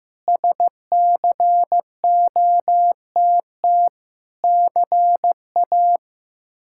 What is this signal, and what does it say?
Morse code, heard as a keyed tone, about 15 wpm, 700 Hz. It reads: SCOTT CA
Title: K1USN SST QSO Practice - 15wpm